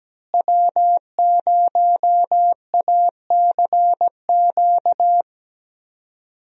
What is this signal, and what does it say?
Morse code, heard as a keyed tone, about 17 wpm, 700 Hz. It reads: W0ACQ
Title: USA Call Signs - 5 Characters - Repeated 3x - 17wpm